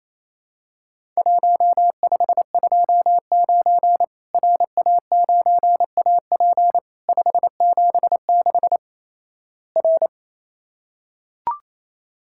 Morse code, heard as a keyed tone, about 28 wpm, 700 Hz. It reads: 1529 RA9AP 576 R
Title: Worked All Europe DX Contest - QTC Exchange - 28wpm